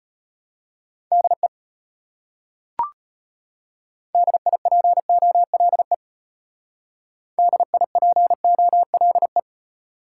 Morse code, heard as a keyed tone, about 33 wpm, 700 Hz. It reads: DE DIPOLE DIPOLE